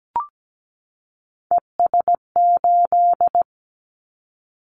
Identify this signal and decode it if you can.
Morse code, heard as a keyed tone, about 17 wpm, 700 Hz. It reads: ES8
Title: Callsign Prefixes for Top 50 DXCC - 17wpm